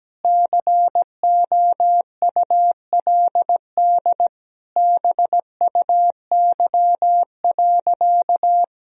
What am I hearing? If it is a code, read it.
Morse code, heard as a keyed tone, about 17 wpm, 700 Hz. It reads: COULD BUY.
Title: Sentences from Top 1000 Words - Review Part 4 - 17wpm